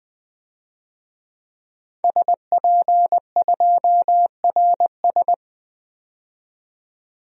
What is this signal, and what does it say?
Morse code, heard as a keyed tone, about 20 wpm, 700 Hz. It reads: SP2RS
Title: DXCC-100 Call Signs - 5 Characters - 20wpm